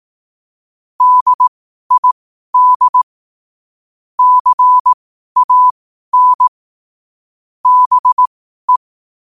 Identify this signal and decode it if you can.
Morse code, heard as a keyed tone, about 18 wpm, spaced as for 12 wpm, 1000 Hz. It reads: DID CAN BE
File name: qotdfiles / 12wpm